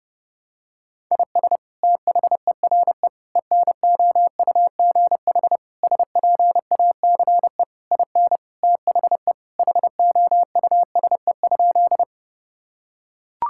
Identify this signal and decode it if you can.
Morse code, heard as a keyed tone, about 30 wpm, 700 Hz. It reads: IS THERE ENOUGH SPACE IN THE HOUSE?
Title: Sentences from Top 1000 Words - Review Part 5 - 30wpm